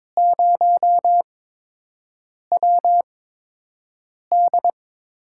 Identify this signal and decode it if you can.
Morse code, heard as a keyed tone, about 22 wpm, spaced as for 6 wpm, 700 Hz. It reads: 0WD
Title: USA Call Signs - 4 Characters - 8x Character Spacing - 22wpm